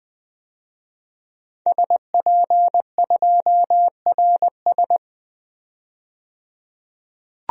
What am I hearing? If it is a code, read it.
Morse code, heard as a keyed tone, about 20 wpm, 700 Hz. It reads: SP2RS
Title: DXCC-100 Call Signs - 5 Characters - 20wpm